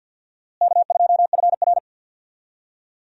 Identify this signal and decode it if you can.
Morse code, heard as a keyed tone, about 52 wpm, 700 Hz. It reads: K2FR
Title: USA Call Signs - 4 Characters - Repeated 2x - 50wpm